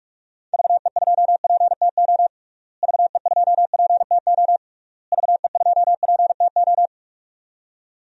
Morse code, heard as a keyed tone, about 41 wpm, 700 Hz. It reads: VE2PTO VE2PTO VE2PTO
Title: DXCC-100 Call Signs - 6 Characters - Repeated 3x - 45wpm